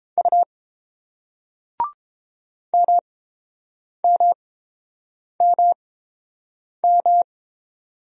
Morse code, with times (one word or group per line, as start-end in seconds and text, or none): 0.2-0.4 U
2.7-7.2 MMMM